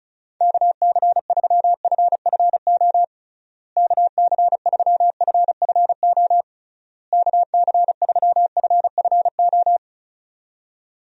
Morse code, heard as a keyed tone, about 36 wpm, 700 Hz. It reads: KC3FFO KC3FFO KC3FFO
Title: USA Call Signs - 6 Characters - Repeated 3x - 35wpm